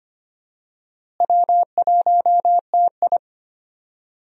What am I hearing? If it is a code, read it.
Morse code, heard as a keyed tone, about 25 wpm, 700 Hz. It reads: W1TI
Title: USA Call Signs - 4 Characters - Repeated 2x - 25wpm